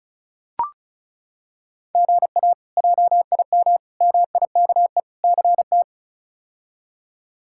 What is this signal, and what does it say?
Morse code, heard as a keyed tone, about 36 wpm, 700 Hz. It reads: GA JIM MIKE CT
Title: K1USN SST QSO Practice - 35wpm